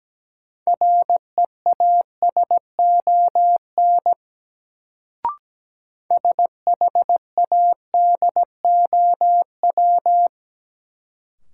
Morse code, heard as a keyed tone, about 17 wpm, 700 Hz. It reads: REASON SHADOW
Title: Sets of 5 Words Encore - New Word Drill - 17wpm